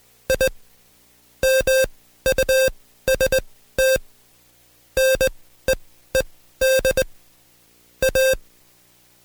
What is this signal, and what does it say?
Morse code, heard as a keyed tone, about 20 wpm, spaced as for 13 wpm, 530 Hz. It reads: I MUST NEED A
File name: Code Quick / AudioDownld2 / Seg133-Stepper6 13WPM